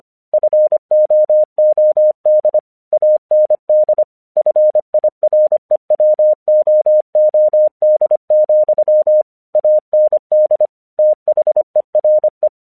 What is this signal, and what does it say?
Morse code, heard as a keyed tone, about 25 wpm, 610 Hz. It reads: FOOD AND FIREWOOD, AND THERE
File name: 1 English 25wpm / AQC05